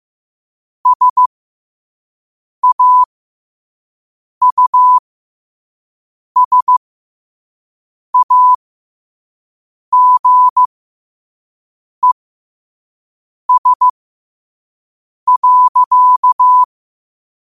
Morse code, heard as a keyed tone, about 15 wpm, spaced as for 5 wpm, 1000 Hz. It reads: SAUSAGES.
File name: qotdfiles / 5wpm